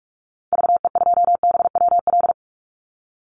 Morse code, heard as a keyed tone, about 44 wpm, 700 Hz. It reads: VE2BWL
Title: Call Signs Repeated 3x - 45wpm